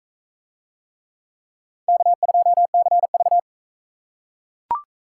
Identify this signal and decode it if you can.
Morse code, heard as a keyed tone, about 42 wpm, 700 Hz. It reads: K2CV